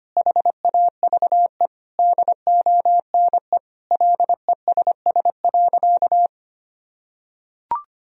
Morse code, heard as a keyed tone, about 25 wpm, 700 Hz. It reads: HAVE DONE LESS.
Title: Sentences from Top 1000 Words - Review Part 3 - 25wpm